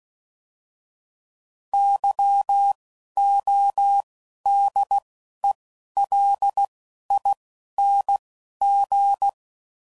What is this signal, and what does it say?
Morse code, heard as a keyed tone, about 16 wpm, 800 Hz. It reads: Y O D E L I N G
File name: Suffixes / Special INGs / Single INGs